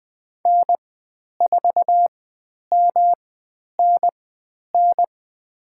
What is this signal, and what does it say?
Morse code, heard as a keyed tone, about 20 wpm, spaced as for 10 wpm, 700 Hz. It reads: N4MNN
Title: 109 Lesson 18g - Letter m - Call Signs 20wpm